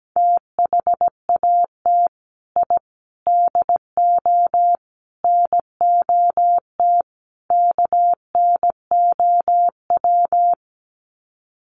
Morse code, heard as a keyed tone, about 17 wpm, 700 Hz. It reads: THAT I DO NOT KNOW
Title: Sets of 5 Words Encore2 17wpm